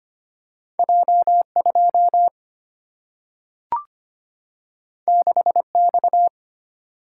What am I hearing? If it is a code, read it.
Morse code, heard as a keyed tone, about 25 wpm, 700 Hz. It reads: J2 6X
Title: Single-Letter Single-Number 25wpm